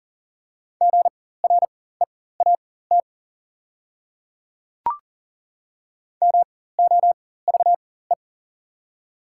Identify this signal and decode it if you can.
Morse code, heard as a keyed tone, about 40 wpm, spaced as for 19 wpm, 700 Hz. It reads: GREAT MOVE
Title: 4x Character Spacing - Top 200 Words - 40wpm